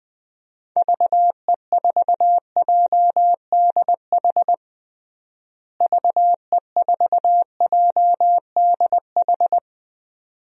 Morse code, heard as a keyed tone, about 21 wpm, 700 Hz. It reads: VE4JDH VE4JDH
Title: DXCC-100 Call Signs - 6 Characters - Repeated 2x - 20wpm